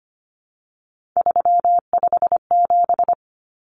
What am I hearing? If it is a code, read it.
Morse code, heard as a keyed tone, about 25 wpm, 700 Hz. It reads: 357
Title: Common Cut Numbers - 3-Digit 25wpm